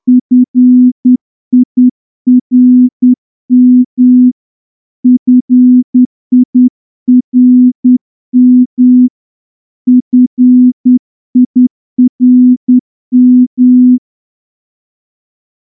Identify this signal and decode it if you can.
Morse code, heard as a keyed tone, about 9 wpm, 250 Hz. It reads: FIRM FIRM FIRM
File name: morse-training-10WPM